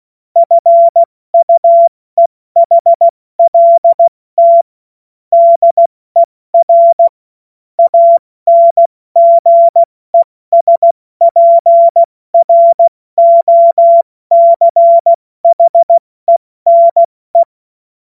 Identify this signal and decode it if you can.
Morse code, heard as a keyed tone, about 16 wpm, 680 Hz. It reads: FUEHLT DER ANGESPROCHENE
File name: Morsekurs / Witze / Witz 15. Februar 2022 70 F80